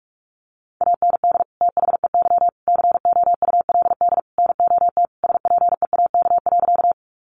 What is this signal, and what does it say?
Morse code, heard as a keyed tone, about 44 wpm, 700 Hz. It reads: AND THEY COULD NOT SPEAK.
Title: Sentences from Top 1100 Words 45wpm